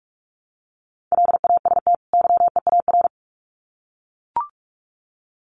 Morse code, heard as a keyed tone, about 44 wpm, 700 Hz. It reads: LAST YEAR
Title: Sets of 2 Words 45wpm